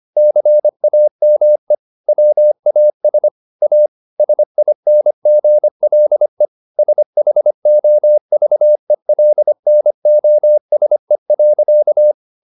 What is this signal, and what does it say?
Morse code, heard as a keyed tone, about 25 wpm, 600 Hz. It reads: CAME WAS A SINGLE SHOVELNOSE.